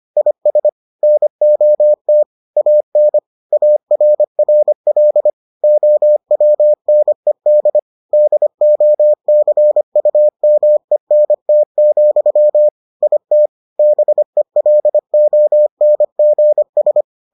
Morse code, heard as a keyed tone, about 25 wpm, 600 Hz. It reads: IS NOT AN ARRL OWNED DOCUMENT, IT BELONGS